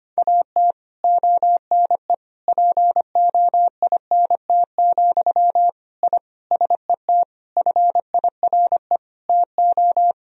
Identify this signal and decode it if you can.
Morse code, heard as a keyed tone, about 25 wpm, 700 Hz. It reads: AT ONE POINT, I SET FIRE TO
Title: Sentences from Top 1000 Words - Review Part 1 - 25wpm